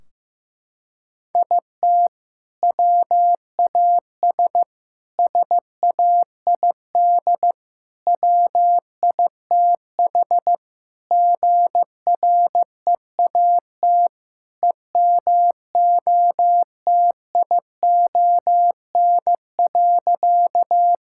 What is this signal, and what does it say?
Morse code, heard as a keyed tone, about 15 wpm, 700 Hz. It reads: IT WAS SAID WITH GREAT EMOTION.
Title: Sentences from Top 1600 Words 15wpm